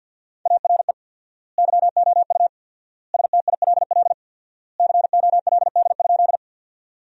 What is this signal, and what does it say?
Morse code, heard as a keyed tone, about 50 wpm, 700 Hz. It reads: ARE YOU STILL COLD?
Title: 4x Word-Spacing - Sentences from Top 400 Words - 50wpm